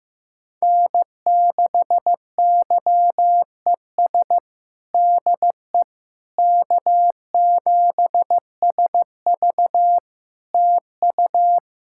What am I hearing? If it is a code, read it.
Morse code, heard as a keyed tone, about 15 wpm, 700 Hz. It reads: N6YES DE K7SV TU
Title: K1USN SST QSO Practice 15wpm